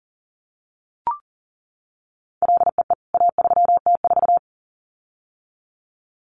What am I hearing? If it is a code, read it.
Morse code, heard as a keyed tone, about 40 wpm, 700 Hz. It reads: LEE A3T4
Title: CWT Exchange 40wpm